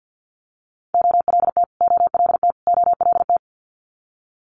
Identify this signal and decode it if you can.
Morse code, heard as a keyed tone, about 50 wpm, 700 Hz. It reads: OLT OLT OLT
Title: Multi-letter Phonemes Repeated 3x 50wpm